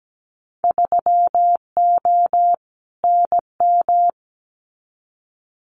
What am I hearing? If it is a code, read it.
Morse code, heard as a keyed tone, about 17 wpm, 700 Hz. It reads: 3O NM
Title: Winter Field Day Exchanges Encore 17wpm